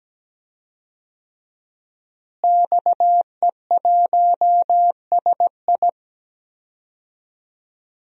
Morse code, heard as a keyed tone, about 17 wpm, 700 Hz. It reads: XE1SI
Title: DXCC-100 Call Signs - 5 Characters - Repeated 3x - 17wpm